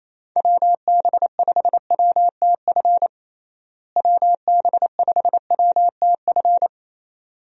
Morse code, heard as a keyed tone, about 28 wpm, 700 Hz. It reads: WB5WTF WB5WTF
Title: USA Call Signs - 6 Characters - Repeated 2x - 28wpm